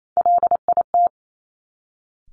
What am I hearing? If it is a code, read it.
Morse code, heard as a keyed tone, about 28 wpm, 690 Hz. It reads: LIT
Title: Top Three Letter Combinations in Words 28wpm